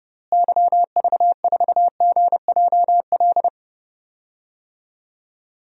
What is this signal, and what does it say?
Morse code, heard as a keyed tone, about 30 wpm, 700 Hz. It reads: YV4GJL
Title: DXCC-100 Call Signs - 6 Characters - Repeated 2x - 30wpm